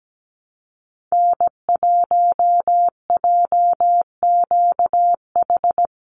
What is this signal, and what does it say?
Morse code, heard as a keyed tone, about 17 wpm, 700 Hz. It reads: N1JQH
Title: USA Call Signs 17wpm